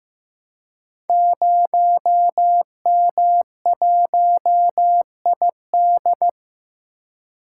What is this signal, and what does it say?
Morse code, heard as a keyed tone, about 15 wpm, 700 Hz. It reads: 0M1ID
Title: Code Groups - 5 Letter:Numbers - 15wpm